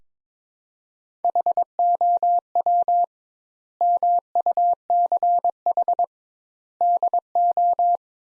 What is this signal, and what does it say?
Morse code, heard as a keyed tone, about 22 wpm, 700 Hz. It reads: HOW MUCH DO
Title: Sentences Top 200 Words - 22wpm